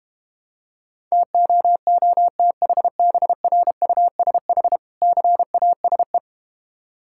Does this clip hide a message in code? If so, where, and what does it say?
Yes: Morse code from 1.1 to 6.2 s, heard as a keyed tone, about 32 wpm, 700 Hz. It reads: TOOTHBRUSH CASE